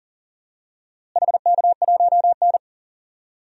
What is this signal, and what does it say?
Morse code, heard as a keyed tone, about 40 wpm, 700 Hz. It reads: HK1N
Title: DXCC-100 Call Signs - 4 Characters - Repeated 3x - 40wpm